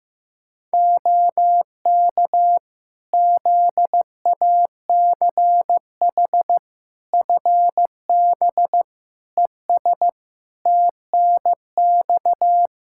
Morse code, heard as a keyed tone, about 16 wpm, 700 Hz. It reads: OK ZACH FB ES TNX
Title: K1USN SST QSO Practice 15wpm